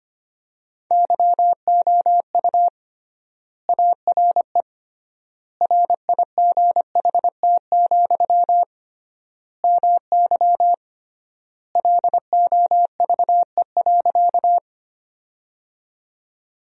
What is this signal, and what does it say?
Morse code, heard as a keyed tone, about 25 wpm, 700 Hz. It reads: YOU ARE RIGHT, MY LOVE.
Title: 3x Word-Spacing - Sentences from Top 400 Words - 25wpm